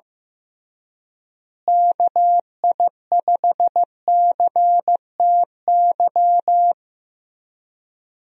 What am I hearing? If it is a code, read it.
Morse code, heard as a keyed tone, about 15 wpm, 700 Hz. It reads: KI5CTY
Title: USA Call Signs - 6 Characters - Repeated 3x - 15wpm